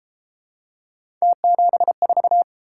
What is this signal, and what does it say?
Morse code, heard as a keyed tone, about 32 wpm, 700 Hz. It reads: T74